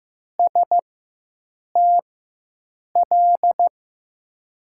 Morse code, heard as a keyed tone, about 15 wpm, spaced as for 7 wpm, 700 Hz. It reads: STL
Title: USA Call Signs - 4x Character Spacing - 15wpm